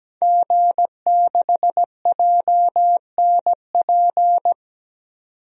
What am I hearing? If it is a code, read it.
Morse code, heard as a keyed tone, about 17 wpm, 700 Hz. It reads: G6JNP
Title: DXCC-100 Call Signs - 5 Characters - 17wpm